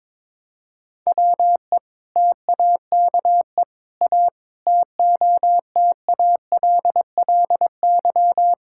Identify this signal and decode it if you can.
Morse code, heard as a keyed tone, about 22 wpm, 700 Hz. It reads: WE TAKE A TOTALLY
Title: Sentences from Top 1600 Words 22wpm